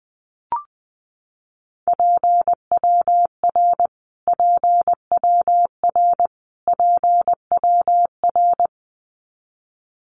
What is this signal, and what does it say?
Morse code, heard as a keyed tone, about 20 wpm, 700 Hz. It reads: PWR PWR PWR
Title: Abbreviations Repeated 3x - 20wpm